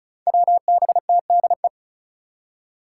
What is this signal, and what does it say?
Morse code, heard as a keyed tone, about 34 wpm, 700 Hz. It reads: WBTDE
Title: Five Challenging Letter-Number Combinations 35wpm